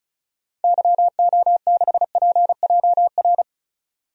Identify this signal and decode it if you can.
Morse code, heard as a keyed tone, about 36 wpm, 700 Hz. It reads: YO6PJR
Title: DXCC-100 Call Signs - 6 Characters - 35wpm